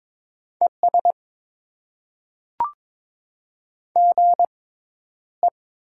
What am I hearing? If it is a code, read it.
Morse code, heard as a keyed tone, about 22 wpm, spaced as for 11 wpm, 700 Hz. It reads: ES GE
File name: Top 20 QSO Elements - 6x Character Spacing - 22wpm